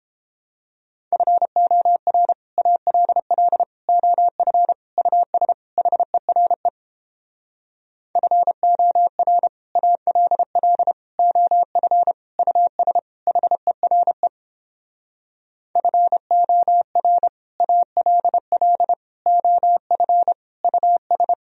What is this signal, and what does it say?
Morse code, heard as a keyed tone, about 29 wpm, 700 Hz. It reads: FOR ALL OF US HERE FOR ALL OF US HERE FOR ALL OF US